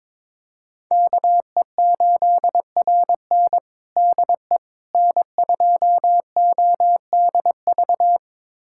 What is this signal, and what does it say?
Morse code, heard as a keyed tone, about 21 wpm, 700 Hz. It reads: KE8RN DE N2ODV
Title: USA Callsign DE Callsign - 5 Characters - 22wpm